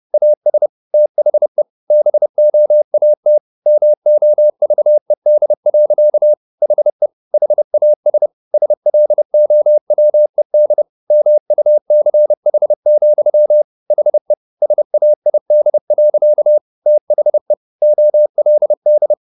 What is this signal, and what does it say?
Morse code, heard as a keyed tone, about 30 wpm, 600 Hz. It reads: AS THE BOAT MOVED. HE HAS SLOWED MUCH, HE SAID. THE OLD
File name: the old man and the sea / the old man and the sea 1 30wpm